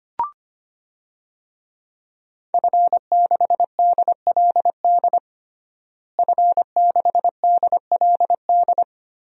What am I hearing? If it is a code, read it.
Morse code, heard as a keyed tone, about 25 wpm, 700 Hz. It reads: F6DLD F6DLD
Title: DXCC-100 Call Signs - 5 Characters - Repeated 3x - 25wpm